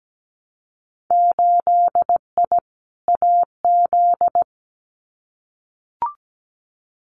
Morse code, heard as a keyed tone, about 17 wpm, 700 Hz. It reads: 8I AZ
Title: Winter Field Day Exchanges Encore 17wpm